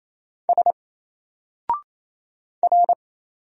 Morse code, heard as a keyed tone, about 27 wpm, 700 Hz. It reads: S R
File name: Single-Letter-Number-28wpm